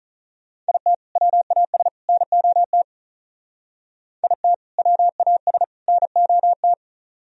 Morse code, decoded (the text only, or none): IT WAS NOT IT WAS NOT